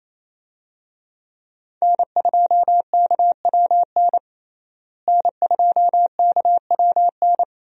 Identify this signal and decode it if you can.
Morse code, heard as a keyed tone, about 28 wpm, 700 Hz. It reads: N2KWN N2KWN
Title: USA Call Signs - 5 Characters - Repeated 3x - 28wpm